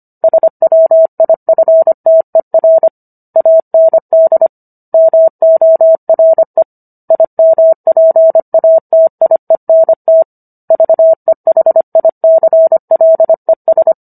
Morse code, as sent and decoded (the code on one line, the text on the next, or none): ... .-- .. ..-. - . .-. / .- -. -.. / -- --- .-. / .. -- .--. .- - .. -. - / ...- . .... .. -.-. .-.. ...
SWIFTER AND MORE IMPATIENT VEHICLES